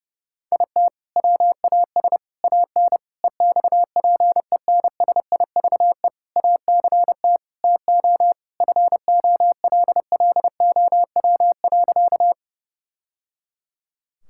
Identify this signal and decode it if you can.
Morse code, heard as a keyed tone, about 30 wpm, 700 Hz. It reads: IT WAS AN EXPENSIVE ACT TO FOLLOW.
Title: Sentences from Top 1000 Words - Review Part 6 - 30wpm